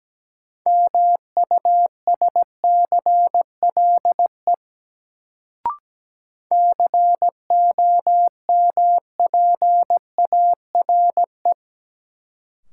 Morse code, heard as a keyed tone, about 17 wpm, 700 Hz. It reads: MUSCLE COMPARE
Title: Sentences from Top 1300 Words - New Word Drill Alt - 17wpm